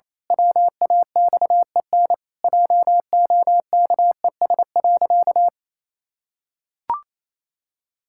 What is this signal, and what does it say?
Morse code, heard as a keyed tone, about 28 wpm, 700 Hz. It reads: WAXEN JOKES.
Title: Pangram Challenge v2 - 28wpm